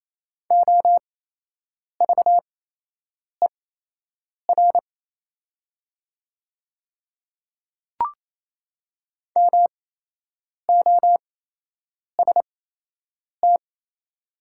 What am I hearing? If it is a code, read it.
Morse code, heard as a keyed tone, about 28 wpm, spaced as for 8 wpm, 700 Hz. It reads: OVER MOST
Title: 8x Character Spacing - Top 100 Words - 28wpm